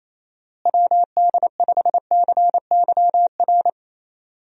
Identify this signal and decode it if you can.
Morse code, heard as a keyed tone, about 28 wpm, 700 Hz. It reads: WD5CYR
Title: USA Call Signs - 6 Characters - Repeated 3x - 28wpm